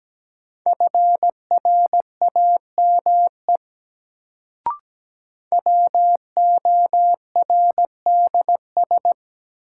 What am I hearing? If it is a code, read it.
Morse code, heard as a keyed tone, about 17 wpm, 700 Hz. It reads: FRAME WORDS
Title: Sentences from Top 1000 Words - Review Part 6 - New Word Drill - 17wpm